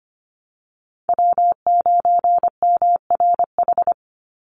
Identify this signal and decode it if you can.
Morse code, heard as a keyed tone, about 25 wpm, 700 Hz. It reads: W9MRH
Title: NA SOTA Chaser Callsigns 25wpm